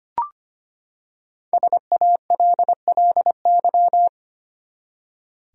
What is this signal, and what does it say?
Morse code, heard as a keyed tone, about 25 wpm, 700 Hz. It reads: SALLY
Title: USA Common Names - Max 5 Letters - 25wpm